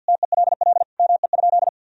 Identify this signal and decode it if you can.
Morse code, heard as a keyed tone, about 52 wpm, 700 Hz. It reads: TELL ME?
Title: Sentences from Top 1000 Words - Review Part 3 - 50wpm